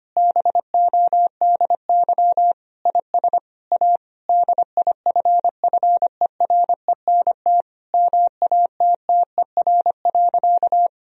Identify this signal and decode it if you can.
Morse code, heard as a keyed tone, about 25 wpm, 700 Hz. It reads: BODY IS A DIFFERENT MATTER.
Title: Sentences from Top 1000 Words - Review Part 3 - 25wpm